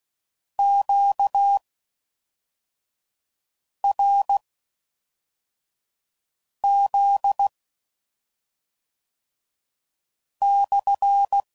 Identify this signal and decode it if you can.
Morse code, heard as a keyed tone, about 16 wpm, spaced as for 3 wpm, 800 Hz. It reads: QRZ/